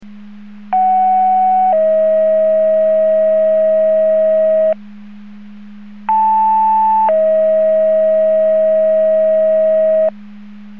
Two-tone page, heard as a one-second tone, then a three-second tone, about 750 then 640 Hz, about 880 then 640 Hz. It is heard from 0.7 to 10.1 s.
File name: Portage 154.385 Paging